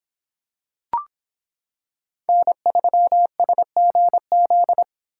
Morse code, heard as a keyed tone, about 26 wpm, 700 Hz. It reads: N3SGZ